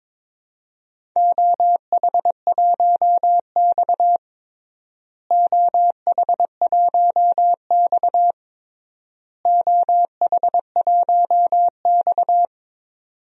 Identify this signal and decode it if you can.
Morse code, heard as a keyed tone, about 22 wpm, 700 Hz. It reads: OH1X OH1X OH1X
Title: DXCC-100 Call Signs - 4 Characters - Repeated 3x - 22wpm